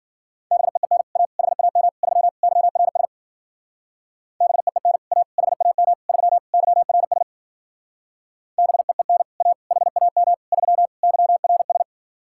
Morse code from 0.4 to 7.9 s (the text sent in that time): BEEN A HAM 3 YRS BEEN A HAM 3 YRS